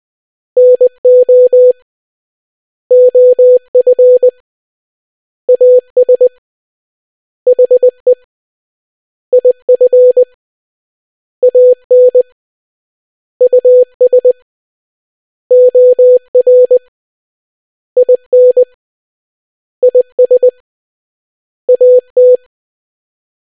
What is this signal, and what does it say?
Morse code, heard as a keyed tone, about 20 wpm, 500 Hz. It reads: NO OF AS HE IF AN US OR IN IS AT